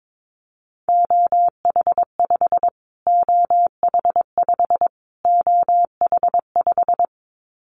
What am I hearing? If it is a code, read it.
Morse code, heard as a keyed tone, about 22 wpm, 700 Hz. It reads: OH5 OH5 OH5
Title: Callsign Prefixes for Top 25 DXCC - Repeated 3x - 22wpm